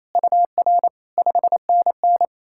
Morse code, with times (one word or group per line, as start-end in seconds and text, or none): 0.2-0.9 UR
1.2-2.2 5NN